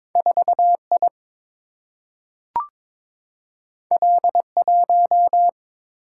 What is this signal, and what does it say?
Morse code, heard as a keyed tone, about 22 wpm, 700 Hz. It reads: 4I L1
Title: Single-Letter Single-Number 22wpm